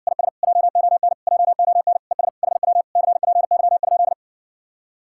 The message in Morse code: IS JON JON ES HW CPY?

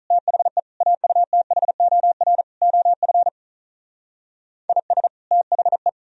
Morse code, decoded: THE AUTHOR OF IS THE

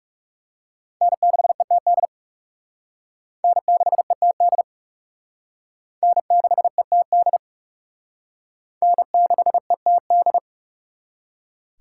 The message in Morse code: N6ETD N6ETD N6ETD N6ETD

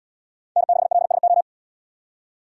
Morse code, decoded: N6LSO

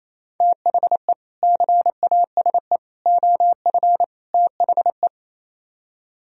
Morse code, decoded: THE CASE OF THE

